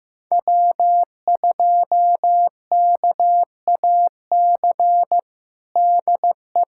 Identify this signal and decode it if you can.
Morse code, heard as a keyed tone, about 15 wpm, 700 Hz. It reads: W2KAC DE
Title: USA Callsign DE Callsign - 5 Characters - 15wpm